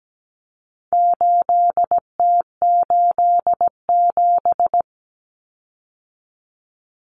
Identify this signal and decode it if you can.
Morse code, heard as a keyed tone, about 17 wpm, 700 Hz. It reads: 8T87
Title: Common Cut Numbers - 4-Digit 17wpm